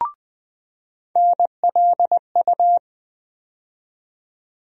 Morse code, heard as a keyed tone, about 20 wpm, 700 Hz. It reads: NLU
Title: USA Call Sign Postfixes - 3 Letters 20wpm